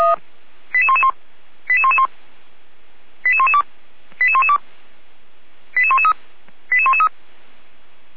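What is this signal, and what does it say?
Five-tone selective call (ZVEI), heard as quick runs of five tones, tones of about 70 ms. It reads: 88111 88111 88112 88112 88113 88113
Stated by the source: ZVEI five-tone selective call received on SDR.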